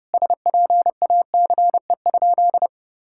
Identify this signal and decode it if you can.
Morse code, heard as a keyed tone, about 30 wpm, 700 Hz. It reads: SPACE?
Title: Sentences from Top 1000 Words - Review Part 5 - 30wpm